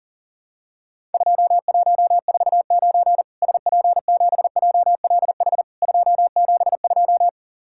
Morse code, heard as a keyed tone, about 40 wpm, 700 Hz. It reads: 2149 SP7JLH 272
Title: Worked All Europe DX Contest - QTC Exchange - 40wpm